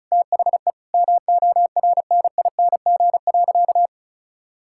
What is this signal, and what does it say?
Morse code, heard as a keyed tone, about 36 wpm, 700 Hz. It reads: THE MORNING.
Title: Sentences from Top 1000 Words - Review Part 6 - 35wpm